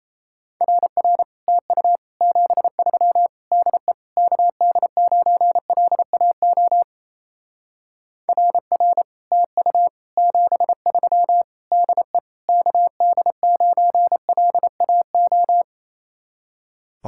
Morse code, heard as a keyed tone, about 34 wpm, 700 Hz. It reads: RR TU 73 DE KD9LAO RR TU 73 DE KD9LAO
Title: Speed-Racing Common QSO Phrases - 28wpm